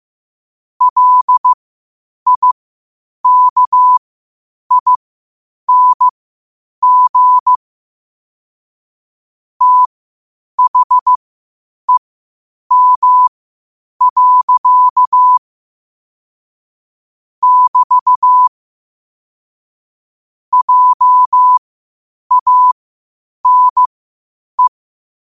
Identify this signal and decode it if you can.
Morse code, heard as a keyed tone, about 15 wpm, spaced as for 8 wpm, 1000 Hz. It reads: LIKING THEM. <BT> JANE
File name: qotdfiles / 8wpm